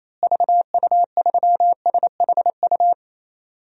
Morse code, heard as a keyed tone, about 28 wpm, 700 Hz. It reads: VU3SHU
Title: DXCC-100 Call Signs - 6 Characters - Repeated 2x - 28wpm